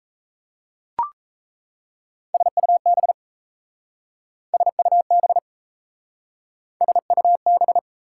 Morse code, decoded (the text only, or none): SUB SUB SUB